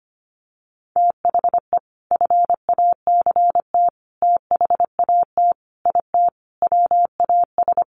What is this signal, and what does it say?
Morse code, heard as a keyed tone, about 25 wpm, 700 Hz. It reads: THE FACT THAT IT WAS
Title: Sets of 5 Words Encore2 25wpm